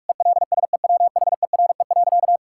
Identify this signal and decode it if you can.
Morse code, heard as a keyed tone, about 47 wpm, 700 Hz. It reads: ELSEWHERE.
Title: Sentences from Top 1600 Words 45wpm